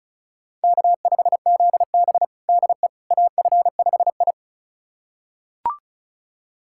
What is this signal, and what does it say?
Morse code, heard as a keyed tone, about 36 wpm, 700 Hz. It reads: K5ZB DE AF5I
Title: USA Callsign DE Callsign - 4 Characters - 35wpm